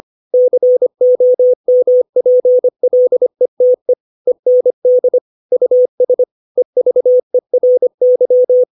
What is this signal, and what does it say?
Morse code, heard as a keyed tone, about 25 wpm, 500 Hz. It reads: COMPLETE END US EVERY
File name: 632 words / 632 Words 25WPM